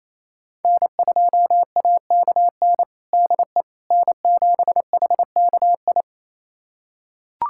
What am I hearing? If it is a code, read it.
Morse code, heard as a keyed tone, about 28 wpm, 700 Hz. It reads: N2AKN DE N7HKI
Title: USA Callsign DE Callsign - 5 Characters - 28wpm